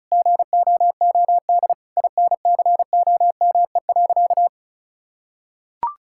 Morse code, heard as a keyed tone, about 36 wpm, 700 Hz. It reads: GOOD INCOME.
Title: Sentences from Top 1000 Words - Review Part 6 - 35wpm